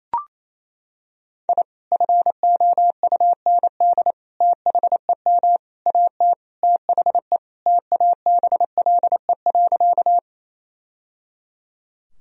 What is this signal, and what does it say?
Morse code, heard as a keyed tone, about 28 wpm, 700 Hz. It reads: I FOUND THEM AT THE TABLE.
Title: Sentences from Top 1000 Words - Review Part 2 - 28wpm